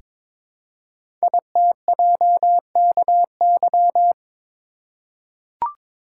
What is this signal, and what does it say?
Morse code, heard as a keyed tone, about 22 wpm, 700 Hz. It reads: ITJKY
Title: Code Groups - 5 Letters - 22wpm